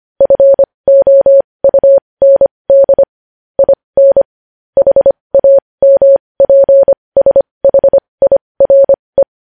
Morse code, heard as a keyed tone, about 25 wpm, spaced as for 20 wpm, 550 Hz. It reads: FOUND IN HAMPSHIRE
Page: News Headlines in Morse Code at 20 WPM